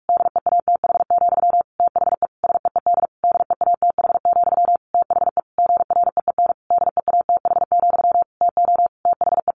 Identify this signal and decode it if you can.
Morse code, heard as a keyed tone, about 46 wpm, 700 Hz. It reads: DEATH, THE SEED DEATH, THE GREEN DEATH, TO THE
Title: A Sound Of Thunder 45wpm